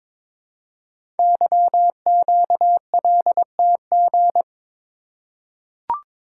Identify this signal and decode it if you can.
Morse code, heard as a keyed tone, about 22 wpm, 700 Hz. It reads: YQLTG
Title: Code Groups - 5 Letters - 22wpm